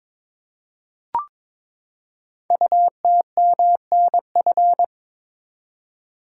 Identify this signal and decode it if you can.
Morse code, heard as a keyed tone, about 22 wpm, 700 Hz. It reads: UTMNF
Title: Code Groups - 5 Letters - 22wpm